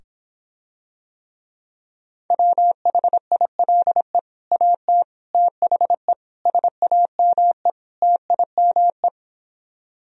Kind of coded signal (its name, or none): Morse code